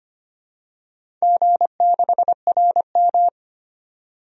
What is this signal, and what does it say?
Morse code, heard as a keyed tone, about 25 wpm, 700 Hz. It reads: G6RM
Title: DXCC-100 Call Signs - 4 Characters - 25wpm